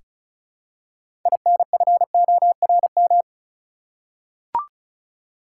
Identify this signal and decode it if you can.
Morse code, heard as a keyed tone, about 36 wpm, 700 Hz. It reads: INFORM
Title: Six Letter Words - 35wpm